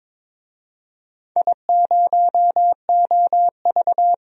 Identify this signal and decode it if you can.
Morse code, heard as a keyed tone, about 22 wpm, 700 Hz. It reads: I0OV
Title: DXCC-100 Call Signs - 4 Characters - 22wpm